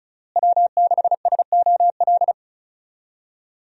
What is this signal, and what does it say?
Morse code, heard as a keyed tone, about 36 wpm, 700 Hz. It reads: W6SOL
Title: USA Call Signs - 5 Characters - Repeated 3x - 35wpm